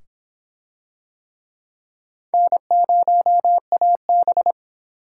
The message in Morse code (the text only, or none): N0AB